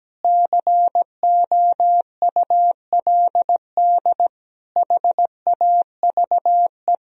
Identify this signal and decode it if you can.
Morse code, heard as a keyed tone, about 17 wpm, 700 Hz. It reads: COULD HAVE
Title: Sentences from Top 1000 Words - Review Part 4 - 17wpm